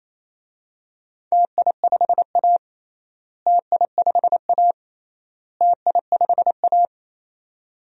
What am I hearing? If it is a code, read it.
Morse code, heard as a keyed tone, about 28 wpm, 700 Hz. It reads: TI5A TI5A TI5A
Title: DXCC-100 Call Signs - 4 Characters - Repeated 3x - 28wpm